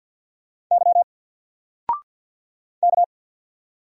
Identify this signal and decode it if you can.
Morse code, heard as a keyed tone, about 50 wpm, 700 Hz. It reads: Y K
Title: Single Letter-Number 50wpm